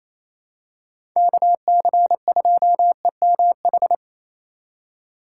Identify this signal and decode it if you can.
Morse code, heard as a keyed tone, about 28 wpm, 700 Hz. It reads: KC2EMH
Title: USA Call Signs - 6 Characters - 4x Character Spacing - 28wpm